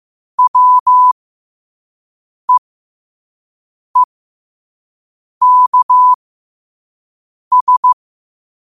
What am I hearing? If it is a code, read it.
Morse code, heard as a keyed tone, about 15 wpm, spaced as for 5 wpm, 1000 Hz. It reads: WEEKS
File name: qotdfiles / 5wpm